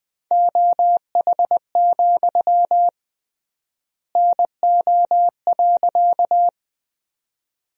Morse code, heard as a keyed tone, about 20 wpm, 700 Hz. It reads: OH, NO.
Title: 3x Word-Spacing - Sentences from Top 500 Words - 20wpm